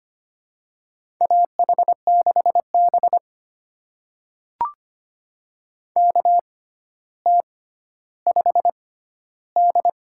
Morse code, heard as a keyed tone, about 25 wpm, spaced as for 13 wpm, 700 Hz. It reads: AH6B KT5D
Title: USA Call Signs - 6x Character Spacing - 25wpm